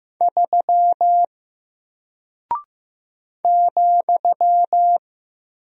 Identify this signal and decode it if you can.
Morse code, heard as a keyed tone, about 15 wpm, 700 Hz. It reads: 3 ,
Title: Single Character (Letter:Number:Punctuation:Prosign) 15wpm